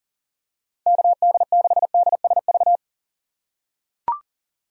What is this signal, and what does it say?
Morse code, heard as a keyed tone, about 40 wpm, 700 Hz. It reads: KD6DSV
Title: USA Call Signs - 6 Characters - 4x Character Spacing - 40wpm